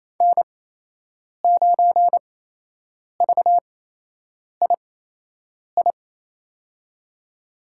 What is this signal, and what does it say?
Morse code, heard as a keyed tone, about 28 wpm, spaced as for 8 wpm, 700 Hz. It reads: N9VII
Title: USA Call Signs - 5 Characters - 8x Character Spacing - 28wpm